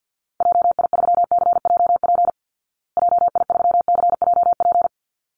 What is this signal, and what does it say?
Morse code, heard as a keyed tone, about 50 wpm, 700 Hz. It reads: JI3CJP JI3CJP
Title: Call Signs Repeated 3x - 50wpm